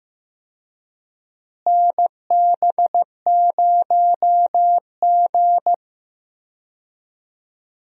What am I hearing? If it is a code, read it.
Morse code, heard as a keyed tone, about 15 wpm, 700 Hz. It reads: NB0G
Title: USA Call Signs - 4 Characters - Repeated 3x - 15wpm